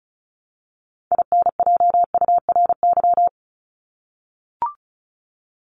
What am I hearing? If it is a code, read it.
Morse code, heard as a keyed tone, about 34 wpm, 700 Hz. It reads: INJURY
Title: Top 1751-2000 Words 35wpm